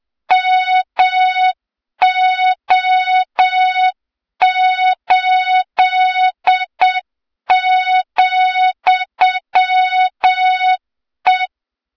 Morse code, heard as a keyed tone, about 7 wpm, 3700 Hz, with noise. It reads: MO8,E
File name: Morze-HA2MN / hang / 23 GYAKORLAT